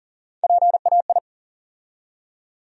Morse code, heard as a keyed tone, about 40 wpm, 700 Hz. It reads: PAI
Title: USA Call Sign Postfixes - 3 Letters 40wpm